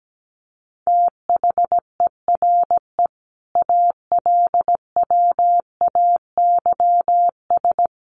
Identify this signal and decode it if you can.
Morse code, heard as a keyed tone, about 17 wpm, 700 Hz. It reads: THERE ALWAYS
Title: Sentences from Top 1100 Words 17wpm